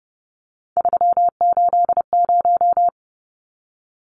Morse code, heard as a keyed tone, about 29 wpm, 700 Hz. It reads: 380
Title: Numbers - 3-Digit 30wpm